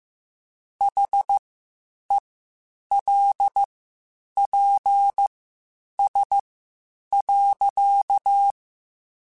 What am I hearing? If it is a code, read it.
Morse code, heard as a keyed tone, about 15 wpm, spaced as for 8 wpm, 800 Hz. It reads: HELPS.